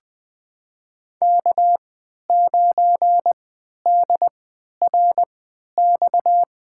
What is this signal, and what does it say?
Morse code, heard as a keyed tone, about 20 wpm, spaced as for 11 wpm, 700 Hz. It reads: K9DRX
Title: USA Call Signs - 3x Character Spacing - 20wpm